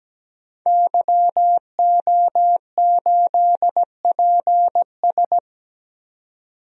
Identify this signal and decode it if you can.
Morse code, heard as a keyed tone, about 17 wpm, 700 Hz. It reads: YO8PS
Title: DXCC-100 Call Signs - 5 Characters - 17wpm